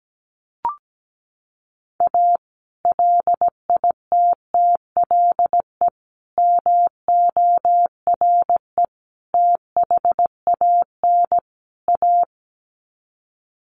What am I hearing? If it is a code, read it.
Morse code, heard as a keyed tone, about 17 wpm, 700 Hz. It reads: A LITTLE MORE THAN A
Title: Sets of 5 Words Encore2 17wpm